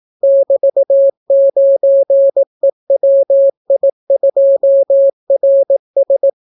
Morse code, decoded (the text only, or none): <BT>9EWI2RS